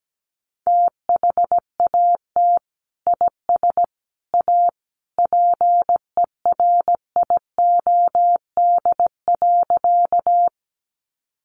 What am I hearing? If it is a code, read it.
Morse code, heard as a keyed tone, about 17 wpm, 700 Hz. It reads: THAT IS A PERIOD.
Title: Sentences from Top 800 Words 17wpm